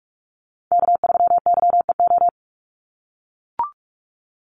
Morse code, heard as a keyed tone, about 44 wpm, 700 Hz. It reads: K3YEO
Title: Call Signs - 45wpm